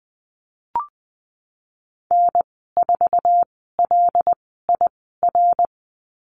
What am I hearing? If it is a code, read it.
Morse code, heard as a keyed tone, about 20 wpm, spaced as for 14 wpm, 700 Hz. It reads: N4LIR
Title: 088 Lesson 15g - Number 5 - Call Signs 20wpm